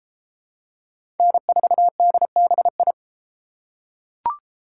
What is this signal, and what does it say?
Morse code, heard as a keyed tone, about 33 wpm, 700 Hz. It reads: N4DBI